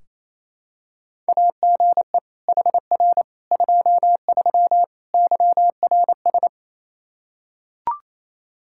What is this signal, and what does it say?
Morse code, heard as a keyed tone, about 28 wpm, 700 Hz. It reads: AGE HR 23 YRS
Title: K1USN SST QSO Practice 28wpm